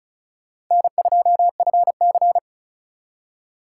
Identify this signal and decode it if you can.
Morse code, heard as a keyed tone, about 45 wpm, 700 Hz. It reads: N2FC